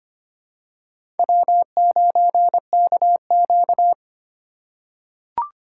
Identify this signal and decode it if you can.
Morse code, heard as a keyed tone, about 25 wpm, 700 Hz. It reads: W9KQ
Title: USA Call Signs - 2x Character Spacing - 25wpm